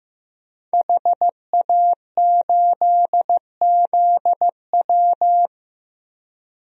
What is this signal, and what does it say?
Morse code, heard as a keyed tone, about 15 wpm, 700 Hz. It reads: HA8ZW
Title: DXCC-100 Call Signs - 5 Characters - Repeated 2x - 15wpm